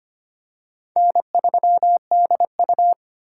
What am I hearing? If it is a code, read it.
Morse code, heard as a keyed tone, about 25 wpm, 700 Hz. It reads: N3DU
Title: USA Call Signs - 4x Character Spacing - 25wpm